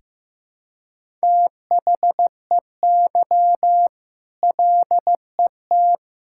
Morse code, heard as a keyed tone, about 15 wpm, 700 Hz. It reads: THEY LET
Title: Sentences from Top 1000 Words - Review Part 5 - 15wpm